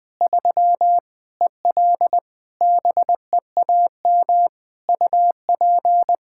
Morse code, heard as a keyed tone, about 20 wpm, 700 Hz. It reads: 3 EL BEAM UP
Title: Practice QSO - Exchange 2 - Info - 20wpm